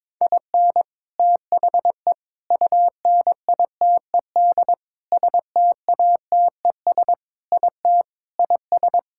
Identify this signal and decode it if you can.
Morse code, heard as a keyed tone, about 22 wpm, 700 Hz. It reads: IN THE UNITED STATES IT IS
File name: practice / 22wpm / Sentences with 4-Word Phrases - 22wpm